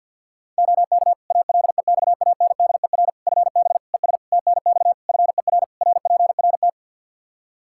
Morse code, heard as a keyed tone, about 52 wpm, 700 Hz. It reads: OK ALEXANDER FB ES TNX FER RPRT
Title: Practice QSO - Exchange 2 - Info - 50wpm